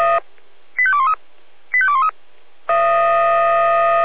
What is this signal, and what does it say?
Five-tone selective call (ZVEI), heard as quick runs of five tones, tones of about 70 ms. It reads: 85213 85213